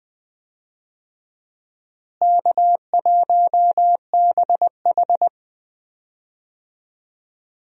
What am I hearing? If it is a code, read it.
Morse code, heard as a keyed tone, about 20 wpm, 700 Hz. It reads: K1BH
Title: USA Call Signs - 4 Characters - Repeated 3x - 20wpm